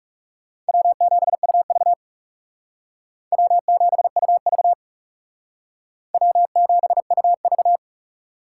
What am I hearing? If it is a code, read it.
Morse code, heard as a keyed tone, about 36 wpm, 700 Hz. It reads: W7UV W7UV W7UV
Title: Speed Racing - USA 1x2 Call Signs 30wpm